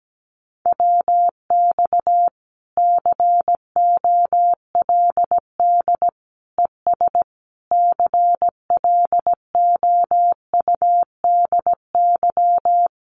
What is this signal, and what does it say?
Morse code, heard as a keyed tone, about 17 wpm, 700 Hz. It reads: WX COLD ES CLOUDY
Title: Common QSO Phrases 17wpm